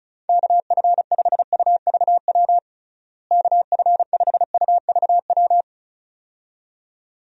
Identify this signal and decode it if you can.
Morse code, heard as a keyed tone, about 34 wpm, 700 Hz. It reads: KF5UVW KF5UVW
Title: USA Call Signs - 6 Characters - Repeated 2x - 35wpm